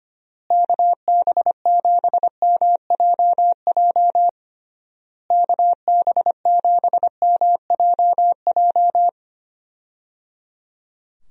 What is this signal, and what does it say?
Morse code, heard as a keyed tone, about 25 wpm, 700 Hz. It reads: KB7MJJ KB7MJJ
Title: USA Call Signs - 6 Characters - Repeated 3x - 25wpm